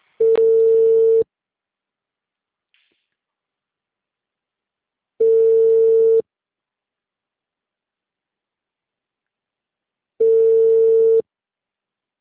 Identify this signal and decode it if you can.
Ringback tone (China, 450 Hz).